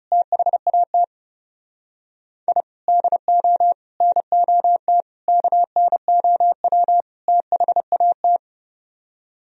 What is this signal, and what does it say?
Morse code, heard as a keyed tone, about 39 wpm, 700 Hz. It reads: THAT I DO NOT KNOW THAT